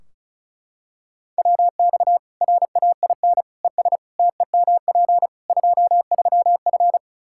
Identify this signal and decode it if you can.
Morse code, heard as a keyed tone, about 36 wpm, 700 Hz. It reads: WX RAIN ES TEMP 23F
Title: Practice QSO - Exchange 2 - Info - 35wpm